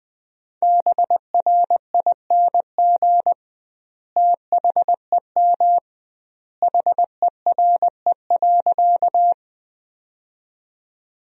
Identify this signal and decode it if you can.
Morse code, heard as a keyed tone, about 20 wpm, 700 Hz. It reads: BRING THEM HERE.
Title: 2x Word-Spacing - Sentences from Top 500 Words - 20wpm